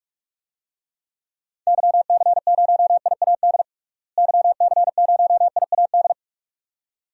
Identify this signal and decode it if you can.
Morse code, heard as a keyed tone, about 47 wpm, 700 Hz. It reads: YC0IAD YC0IAD
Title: DXCC-100 Call Signs - 6 Characters - Repeated 2x - 45wpm